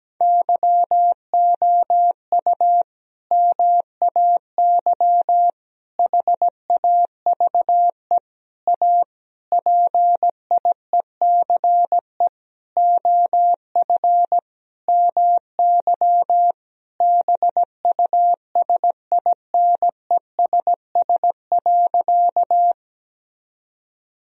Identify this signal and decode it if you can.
Morse code, heard as a keyed tone, about 17 wpm, 700 Hz. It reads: YOU MAY HAVE A PIECE OF MY BUSINESS.
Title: Sentences from Top 1000 Words - Review Part 2 - 17wpm